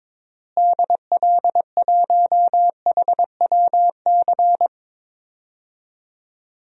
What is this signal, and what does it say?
Morse code, heard as a keyed tone, about 22 wpm, 700 Hz. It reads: DL1HWC
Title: DXCC-100 Call Signs - 6 Characters - Repeated 3x - 22wpm